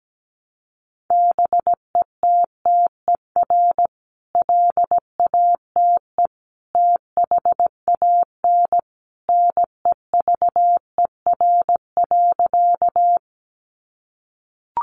Morse code, heard as a keyed tone, about 17 wpm, 700 Hz. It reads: BETTER LATE THAN NEVER.
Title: Sentences from Top 400 Words 17wpm